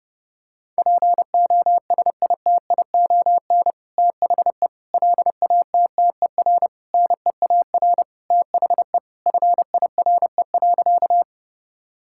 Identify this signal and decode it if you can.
Morse code, heard as a keyed tone, about 30 wpm, 700 Hz. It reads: POSITION THE LATTER NEAR THE FIRE.
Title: Sentences from Top 1600 Words 30wpm